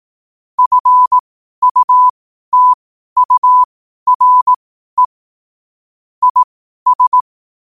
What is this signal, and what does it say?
Morse code, heard as a keyed tone, about 18 wpm, spaced as for 12 wpm, 1000 Hz. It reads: FUTURE IS
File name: qotdfiles / 12wpm